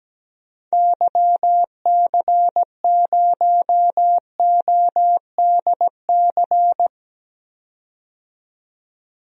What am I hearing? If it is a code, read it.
Morse code, heard as a keyed tone, about 17 wpm, 700 Hz. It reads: YC0ODC
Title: DXCC-100 Call Signs - 6 Characters - 17wpm